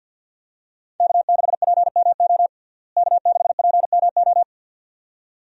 Morse code, heard as a keyed tone, about 50 wpm, 700 Hz. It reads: K6PMO K6PMO
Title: USA Call Signs - 5 Characters - Repeated 2x - 50wpm